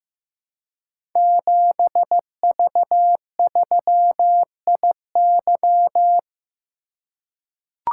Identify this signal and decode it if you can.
Morse code, heard as a keyed tone, about 15 wpm, 700 Hz. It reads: 7V3IY
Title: Code Groups - 5 Letter:Numbers - 15wpm